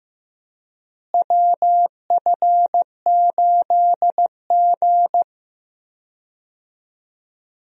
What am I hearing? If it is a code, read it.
Morse code, heard as a keyed tone, about 15 wpm, 700 Hz. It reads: WF8G
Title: USA Call Signs - 4 Characters - Repeated 3x - 15wpm